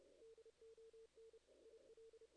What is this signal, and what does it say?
Morse code, heard as a keyed tone, about 31 wpm, 440 Hz. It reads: DON IL
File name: 2016 JAN CW NAQP / 2016 Jan naqp k5ot - 0-11